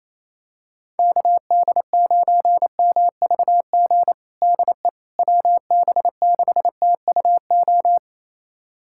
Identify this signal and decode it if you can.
Morse code, heard as a keyed tone, about 28 wpm, 700 Hz. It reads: KD9MVG DE WB6TUO
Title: USA Callsign DE Callsign 28wpm